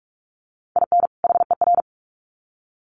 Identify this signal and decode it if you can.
Morse code, heard as a keyed tone, about 44 wpm, 700 Hz. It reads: IN HER
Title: Sets of 2 Words 45wpm